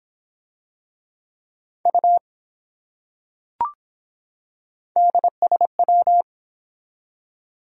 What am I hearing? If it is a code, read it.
Morse code, heard as a keyed tone, about 24 wpm, 700 Hz. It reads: U DSW